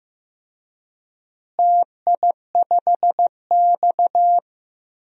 Morse code, heard as a keyed tone, about 15 wpm, 700 Hz. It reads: TI5X
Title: DXCC-100 Call Signs - 4 Characters - Repeated 3x - 15wpm